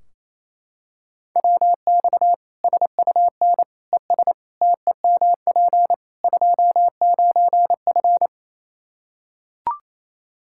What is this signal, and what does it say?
Morse code, heard as a keyed tone, about 28 wpm, 700 Hz. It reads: WX SUN ES TEMP 29F
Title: Practice QSO - Exchange 2 - Info - 28wpm